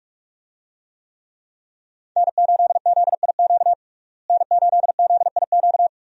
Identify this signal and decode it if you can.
Morse code, heard as a keyed tone, about 46 wpm, 700 Hz. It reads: N8ZIQ N8ZIQ
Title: USA Call Signs - 5 Characters - Repeated 2x - 45wpm